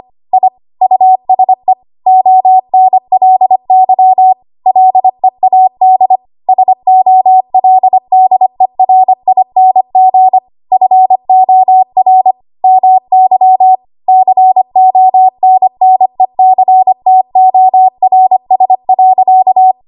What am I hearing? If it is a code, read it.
Morse code, heard as a keyed tone, about 25 wpm, 750 Hz. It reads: I USE ONLY LEAD SOLDERING FOR MY CONNECTORS.
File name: Archive / 25 WPM / 25WPM